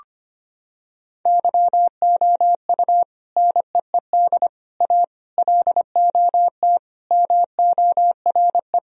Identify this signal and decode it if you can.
Morse code, heard as a keyed tone, about 25 wpm, 700 Hz. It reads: YOU NEED A LOT MORE
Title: Sentences from Top 1000 Words - Review Part 3 - 25wpm